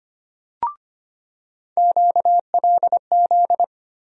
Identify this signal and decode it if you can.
Morse code, heard as a keyed tone, about 25 wpm, 700 Hz. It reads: QLZ
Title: USA Call Sign Postfixes - 3 Letters 25wpm